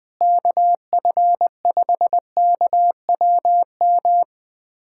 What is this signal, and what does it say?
Morse code, heard as a keyed tone, about 20 wpm, 700 Hz. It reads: KF5KWM
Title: USA Callsign DE Callsign 20wpm